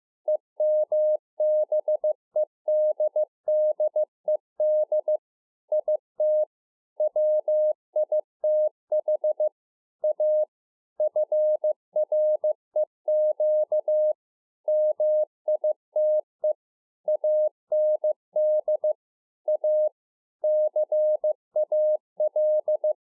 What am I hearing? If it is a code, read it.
Morse code, heard as a keyed tone, about 15 wpm, 610 Hz. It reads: EMBEDDED IT WITH A FREQ MITE AND A CAL
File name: 2011-04-mp3 / A SMALL TRANCEIVER 2 15WPM